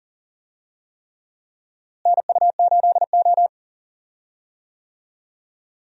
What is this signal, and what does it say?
Morse code, heard as a keyed tone, about 40 wpm, 700 Hz. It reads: NU8O
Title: USA Call Signs - 4 Characters - Repeated 3x - 40wpm